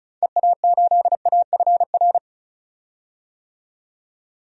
Morse code, heard as a keyed tone, about 34 wpm, 700 Hz. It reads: EA8AFR
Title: DXCC-100 Call Signs - 6 Characters - 35wpm